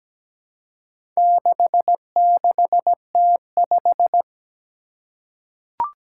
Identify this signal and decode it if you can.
Morse code, heard as a keyed tone, about 17 wpm, 700 Hz. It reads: 66T5
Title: Four Challenging Letter-Number Combinations 17wpm